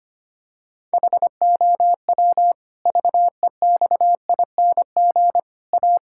Morse code, heard as a keyed tone, about 25 wpm, 700 Hz. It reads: HOW VEXING A
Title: Pangram Challenge v2 - 25wpm